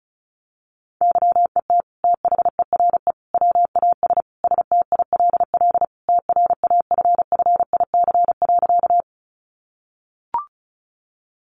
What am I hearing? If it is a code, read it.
Morse code, heard as a keyed tone, about 34 wpm, 700 Hz. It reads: YET THERE WAS STILL TRAFFIC.
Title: Sentences from Top 1300 Words 35wpm